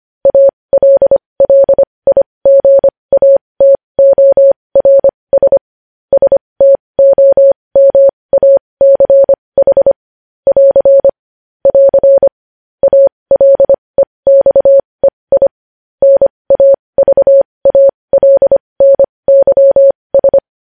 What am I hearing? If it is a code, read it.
Morse code, heard as a keyed tone, about 25 wpm, spaced as for 20 wpm, 550 Hz. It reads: ALLIGATORS STOMACH <AR> <AR> ALEXEI NAVALNYS
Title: News Headlines at 20